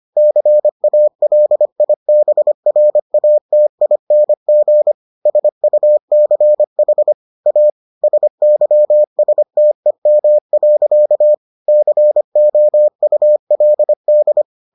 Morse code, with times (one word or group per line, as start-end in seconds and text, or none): 0.2-4.9 CALIBRATING
5.3-7.1 SUCH
7.5-7.7 A
8.0-11.3 SYSTEM.
11.7-14.4 COULD